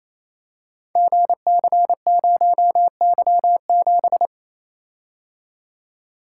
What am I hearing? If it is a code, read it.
Morse code, heard as a keyed tone, about 28 wpm, 700 Hz. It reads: GC0Y7
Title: Code Groups - 5 Letter:Numbers - 28wpm